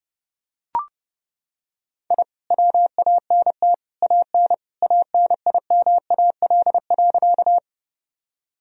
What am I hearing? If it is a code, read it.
Morse code, heard as a keyed tone, about 32 wpm, 700 Hz. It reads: I WANT AN ANIMAL.
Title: Sentences from Top 1000 Words - Review Part 1 - 30wpm